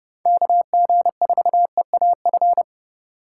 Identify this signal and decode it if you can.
Morse code, heard as a keyed tone, about 40 wpm, 700 Hz. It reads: KG4EAF